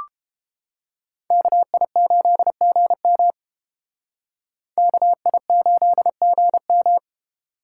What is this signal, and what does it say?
Morse code, heard as a keyed tone, about 29 wpm, 700 Hz. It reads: KI8GM KI8GM